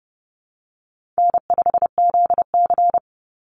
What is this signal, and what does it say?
Morse code, heard as a keyed tone, about 30 wpm, 700 Hz. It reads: N5ZC
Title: NA SOTA Chaser Callsigns 30wpm